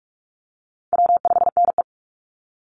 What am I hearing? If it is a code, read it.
Morse code, heard as a keyed tone, about 46 wpm, 700 Hz. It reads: W5NE
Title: Easy Call Signs - 45wpm